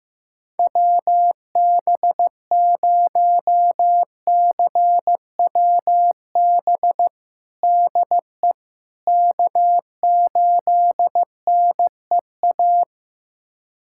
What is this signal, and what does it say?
Morse code, heard as a keyed tone, about 15 wpm, 700 Hz. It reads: WB0CWB DE K8NEA
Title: USA Callsign DE Callsign 15wpm